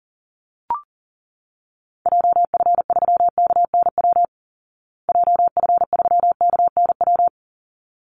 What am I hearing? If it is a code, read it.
Morse code, heard as a keyed tone, about 40 wpm, 700 Hz. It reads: JF3KNW JF3KNW
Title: Call Signs Repeated 3x - 40wpm